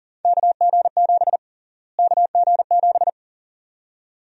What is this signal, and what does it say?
Morse code, heard as a keyed tone, about 40 wpm, 700 Hz. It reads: KG7 KG7
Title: USA Call Sign Prefixes - Repeated 2x - 40wpm